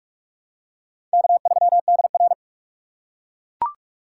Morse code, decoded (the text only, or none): K3DR